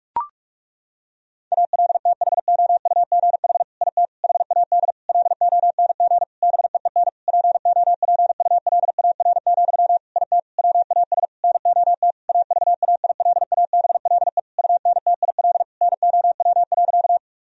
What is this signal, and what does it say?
Morse code, heard as a keyed tone, about 46 wpm, 700 Hz. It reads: ALTHOUGH IT HAD LONG BEEN POPULAR, IT WAS NOT AVAILABLE UNTIL NOW.
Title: Sentences from Top 1000 Words - Review Part 3 - 45wpm